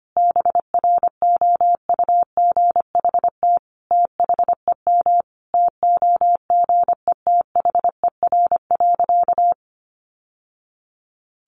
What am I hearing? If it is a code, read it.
Morse code, heard as a keyed tone, about 25 wpm, 700 Hz. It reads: BROUGHT THEM TOGETHER.
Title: Sentences from Top 1000 Words 25wpm